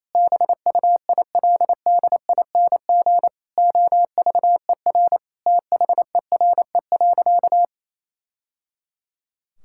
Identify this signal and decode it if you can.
Morse code, heard as a keyed tone, about 28 wpm, 700 Hz. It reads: BUILDING OVER THERE.
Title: Sentences from Top 1000 Words - Review Part 5 - 28wpm